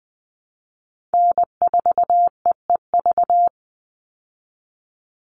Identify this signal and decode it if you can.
Morse code, heard as a keyed tone, about 20 wpm, 700 Hz. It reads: N4EEV
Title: Call Signs Repeated 3x - 20wpm